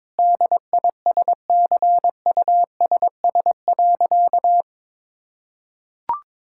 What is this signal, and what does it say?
Morse code, heard as a keyed tone, about 22 wpm, 700 Hz. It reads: DISCUSS.
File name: practice / 22wpm / Sentences with 2-Word Phrases - 22wpm